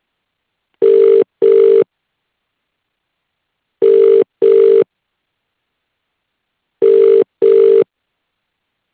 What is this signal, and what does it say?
Ringback tone (UK), 400 + 450 Hz.